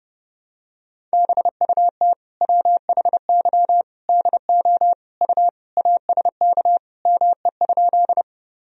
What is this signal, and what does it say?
Morse code, heard as a keyed tone, about 30 wpm, 700 Hz. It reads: BUT WHY DO U ASK ME?
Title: Sentences from Top 200 Words and Abbreviations 30wpm